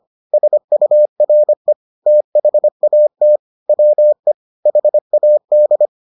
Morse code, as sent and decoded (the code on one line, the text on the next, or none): ... ..- .-. / - .... .- - / .-- . / .... .- -..
SURE THAT WE HAD